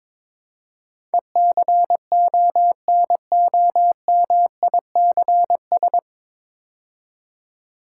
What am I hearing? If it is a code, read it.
Morse code, heard as a keyed tone, about 22 wpm, 700 Hz. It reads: ECONOMICS
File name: 9 Letter Words - First 3 Letters Given - 22wpm